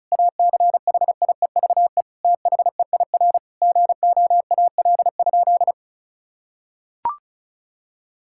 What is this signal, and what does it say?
Morse code, heard as a keyed tone, about 36 wpm, 700 Hz. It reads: ACHIEVE THEIR GOAL?
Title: Sentences from Top 1000 Words - Review Part 6 - 35wpm